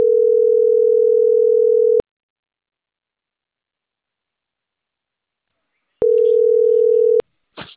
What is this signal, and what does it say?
Ringback tone (North America), 440 + 480 Hz.